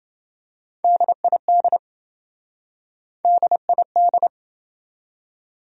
Morse code, heard as a keyed tone, about 28 wpm, 700 Hz. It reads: DID DID